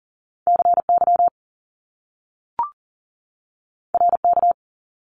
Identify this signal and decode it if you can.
Morse code, heard as a keyed tone, about 39 wpm, 700 Hz. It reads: CY RK
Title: Top Two Letter Combinations in Words 40wpm